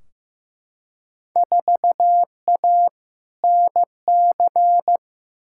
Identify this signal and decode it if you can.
Morse code, heard as a keyed tone, about 15 wpm, 700 Hz. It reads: 4A NC
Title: Field Day Exchanges - 15wpm